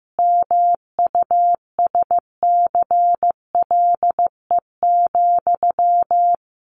Morse code, heard as a keyed tone, about 15 wpm, 700 Hz. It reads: MUSCLE,
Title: A Sound Of Thunder 15wpm